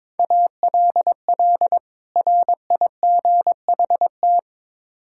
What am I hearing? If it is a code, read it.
Morse code, heard as a keyed tone, about 22 wpm, 700 Hz. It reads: ALL RIGHT
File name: Build A Phrase - 2-Word Phrases - 22wpm